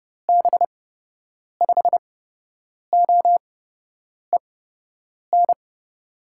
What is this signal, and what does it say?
Morse code, heard as a keyed tone, about 31 wpm, spaced as for 8 wpm, 700 Hz. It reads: B5OEN
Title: USA Call Signs - 6 Characters - 8x Character Spacing - 30wpm